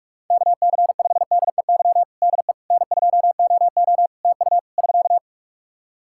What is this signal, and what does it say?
Morse code, heard as a keyed tone, about 46 wpm, 700 Hz. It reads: KC5DEY DE N2OO TU <SK>
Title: K1USN SST QSO Practice 45wpm